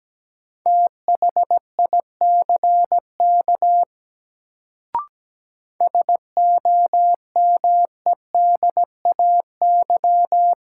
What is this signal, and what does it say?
Morse code, heard as a keyed tone, about 17 wpm, 700 Hz. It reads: THICK SOMEDAY
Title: Sentences from Top 1600 Words - New Word Drill Alt - 17wpm